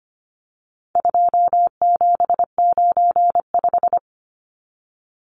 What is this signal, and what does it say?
Morse code, heard as a keyed tone, about 25 wpm, 700 Hz. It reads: 2795
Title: Numbers - 4-Digit 25wpm